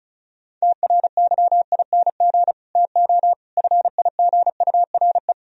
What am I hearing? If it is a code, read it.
Morse code, heard as a keyed tone, about 46 wpm, 700 Hz. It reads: TRYING TO FIGURE